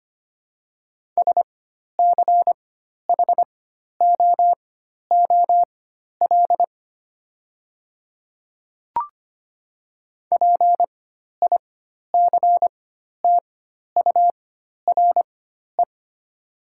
Morse code, heard as a keyed tone, about 25 wpm, spaced as for 12 wpm, 700 Hz. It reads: SCHOOL PICTURE
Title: 4x Character Spacing - Top 500 Words - 25wpm